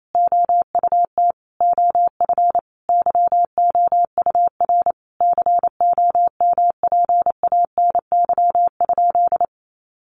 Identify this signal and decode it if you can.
Morse code, heard as a keyed tone, about 28 wpm, 700 Hz. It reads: OUT OF YOUR COMPANY?
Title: Sentences from Top 600 Words 28wpm